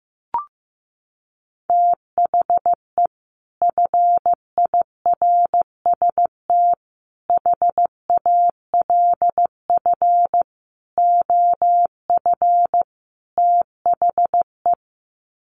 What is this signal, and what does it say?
Morse code, heard as a keyed tone, about 15 wpm, 700 Hz. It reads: THE FIRST HALF OF THE
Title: Sets of 5 Words 15wpm